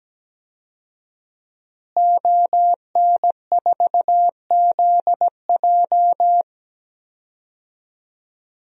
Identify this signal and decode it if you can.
Morse code, heard as a keyed tone, about 17 wpm, 700 Hz. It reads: ON4ZJ
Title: DXCC-100 Call Signs - 5 Characters - Repeated 2x - 17wpm